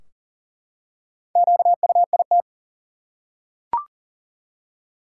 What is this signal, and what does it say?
Morse code, heard as a keyed tone, about 40 wpm, 700 Hz. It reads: QUIT
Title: Four Letter Words - 40wpm